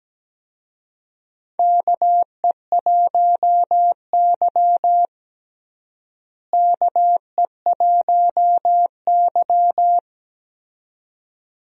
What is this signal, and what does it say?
Morse code, heard as a keyed tone, about 17 wpm, 700 Hz. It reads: KE1Y KE1Y
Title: USA Call Signs - 4 Characters - Repeated 2x - 17wpm